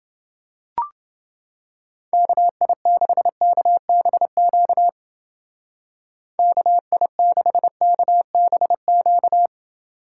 Morse code, heard as a keyed tone, about 28 wpm, 700 Hz. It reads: KI6KBQ KI6KBQ